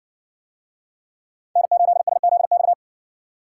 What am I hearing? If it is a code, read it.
Morse code, heard as a keyed tone, about 58 wpm, 700 Hz. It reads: N8SZX